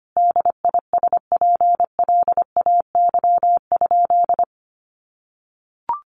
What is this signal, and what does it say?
Morse code, heard as a keyed tone, about 25 wpm, 700 Hz. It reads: DISPLAY?
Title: Sentences from Top 1200 Words 25wpm